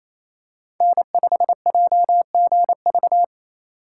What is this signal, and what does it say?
Morse code, heard as a keyed tone, about 28 wpm, 700 Hz. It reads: N5JGV
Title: USA Call Signs - 5 Characters - 2x Character Spacing - 28wpm